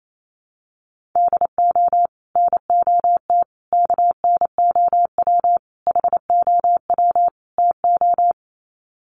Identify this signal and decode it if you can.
Morse code, heard as a keyed tone, about 28 wpm, 700 Hz. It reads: DO NOT KNOW HOW TO
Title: Sets of 5 Words 28wpm